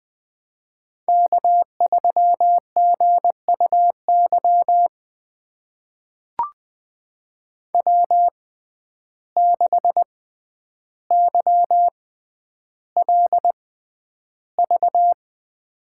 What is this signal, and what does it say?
Morse code, heard as a keyed tone, about 20 wpm, spaced as for 10 wpm, 700 Hz. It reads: K3GUY W6YLV
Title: USA Call Signs - 5 Characters - 6x Character Spacing - 20wpm